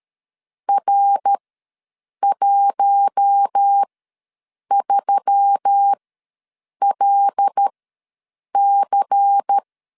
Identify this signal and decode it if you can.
Morse code, heard as a keyed tone, about 13 wpm, spaced as for 7 wpm, 800 Hz. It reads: R13LC
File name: LEZIONI / 5 LBVC / 5 LBVC ws 12wpm 3159